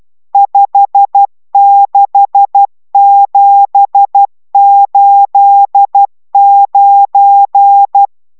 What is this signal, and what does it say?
Morse code, heard as a keyed tone, about 12 wpm, 800 Hz. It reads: 56789